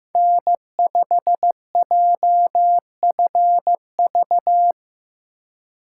Morse code, heard as a keyed tone, about 15 wpm, 700 Hz. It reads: N5JFV
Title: USA Call Signs - 5 Characters - 4x Character Spacing - 15wpm